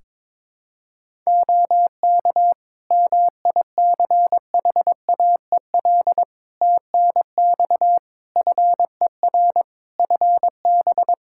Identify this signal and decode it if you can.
Morse code, heard as a keyed tone, about 21 wpm, 700 Hz. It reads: OK MICHAEL TNX FER FB
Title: Practice QSO - Exchange 4 - Ending - 22wpm